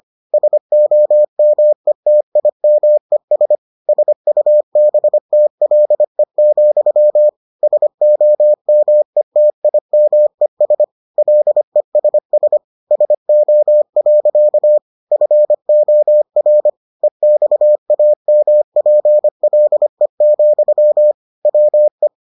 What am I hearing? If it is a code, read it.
Morse code, heard as a keyed tone, about 25 wpm, 600 Hz. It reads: SOMETIMES SUBTLE, SOMETIMES LESS SO. FOR EXAMPLE, WE